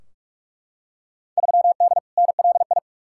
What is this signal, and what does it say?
Morse code, heard as a keyed tone, about 44 wpm, 700 Hz. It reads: 3D NLI
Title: Field Day Exchanges Encore - 45wpm